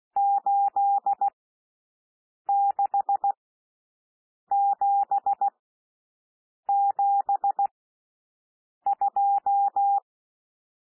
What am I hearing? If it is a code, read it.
Morse code, heard as a keyed tone, about 16 wpm, spaced as for 6 wpm, 800 Hz. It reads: 86772
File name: SOUND NUMBERS 16 wpm